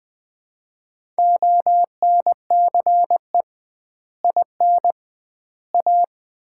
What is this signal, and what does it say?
Morse code, heard as a keyed tone, about 19 wpm, 700 Hz. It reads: ONCE IN A
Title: 2x Word-Spacing - Sets of 4 Words - 20wpm